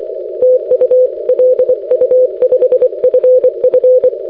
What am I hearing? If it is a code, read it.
Morse code, heard as a keyed tone, about 24 wpm, 510 Hz. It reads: TU LU5FF
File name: 6m-lu5ff